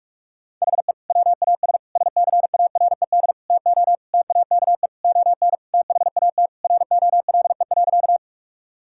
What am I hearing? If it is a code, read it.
Morse code, heard as a keyed tone, about 46 wpm, 700 Hz. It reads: HE WAS SCARED TO TAKE ON THAT ROLE.
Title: Sentences from Top 1000 Words - Review Part 6 - 45wpm